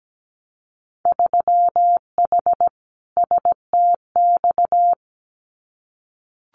Morse code, heard as a keyed tone, about 17 wpm, 700 Hz. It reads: 3H STX
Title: Winter Field Day Exchanges Encore 17wpm